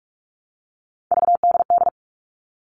Morse code, heard as a keyed tone, about 44 wpm, 700 Hz. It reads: VDD
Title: All Cut Numbers - 3-Digit 45wpm